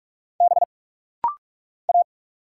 Morse code, heard as a keyed tone, about 46 wpm, 690 Hz. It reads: B A
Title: Single Letters - Rapid-Fire 45wpm